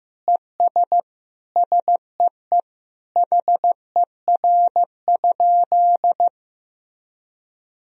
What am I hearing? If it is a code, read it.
Morse code, heard as a keyed tone, about 16 wpm, 700 Hz. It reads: ES SEE HER?
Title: Sentences from Top 100 Words and Abbreviations 15wpm